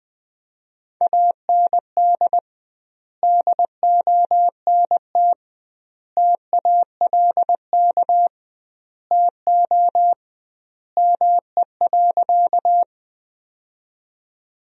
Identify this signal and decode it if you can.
Morse code, heard as a keyed tone, about 20 wpm, 700 Hz. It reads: AND DONT TALK TO ME.
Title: 2x Word-Spacing - Sentences from Top 400 Words - 20wpm